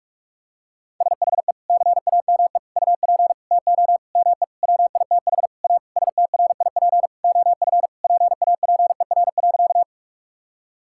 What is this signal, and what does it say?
Morse code, heard as a keyed tone, about 41 wpm, 700 Hz. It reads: SHE CAME UP TO ME WITH A STRIP OF PAPER.
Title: Sentences from Top 1600 Words 45wpm